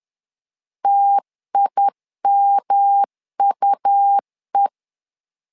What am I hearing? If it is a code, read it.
Morse code, heard as a keyed tone, about 11 wpm, 800 Hz. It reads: TIMUE
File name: LEZIONI / 3 MWRO / 3 M 10wpm 1828